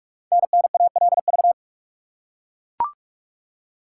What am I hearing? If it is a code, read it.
Morse code, heard as a keyed tone, about 44 wpm, 700 Hz. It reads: NNALV
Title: Code Groups - 5 Letter:Numbers - 45wpm